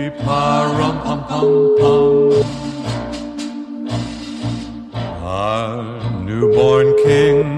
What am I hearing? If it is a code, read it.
Ringback tone (Europe, 425 Hz), with noise.